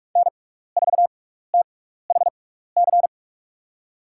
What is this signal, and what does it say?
Morse code, heard as a keyed tone, about 46 wpm, spaced as for 15 wpm, 700 Hz. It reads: N4THC
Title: USA Call Signs - 5 Characters - 6x Character Spacing - 45wpm